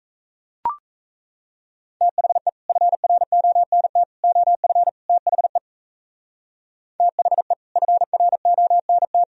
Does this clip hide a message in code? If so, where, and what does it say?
Yes: Morse code from 2.0 to 9.2 s, heard as a keyed tone, about 38 wpm, 700 Hz. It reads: THE FRONT OF THE THE FRONT